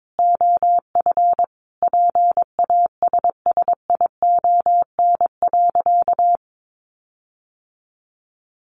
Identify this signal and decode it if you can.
Morse code, heard as a keyed tone, about 22 wpm, 700 Hz. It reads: OF PASSION.
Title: Sentences from Top 1300 Words 22wpm